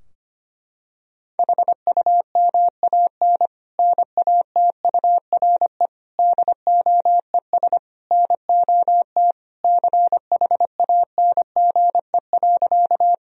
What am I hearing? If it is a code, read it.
Morse code, heard as a keyed tone, about 25 wpm, 700 Hz. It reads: HUMAN NATURE DOES NOT CHANGE.
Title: Sentences from Top 1000 Words - Review Part 3 - 25wpm